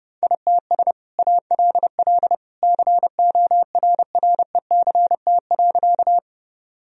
Morse code, heard as a keyed tone, about 30 wpm, 700 Hz. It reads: ITS ALL CORRECT.
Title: Sentences from Top 1000 Words - Review Part 2 - 30wpm